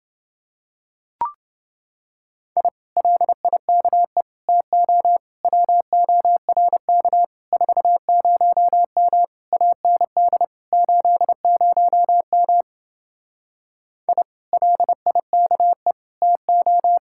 Morse code, heard as a keyed tone, about 27 wpm, 700 Hz. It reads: I LIKE TO WORK 40M AND 80M I LIKE TO